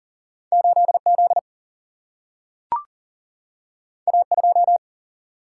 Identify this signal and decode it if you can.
Morse code, heard as a keyed tone, about 40 wpm, 700 Hz. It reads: 8 Z A 2
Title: Single-Letter Single-Number 40wpm